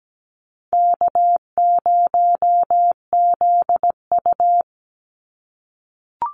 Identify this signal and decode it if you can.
Morse code, heard as a keyed tone, about 17 wpm, 700 Hz. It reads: K0ZU
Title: Easy Call Signs - 17wpm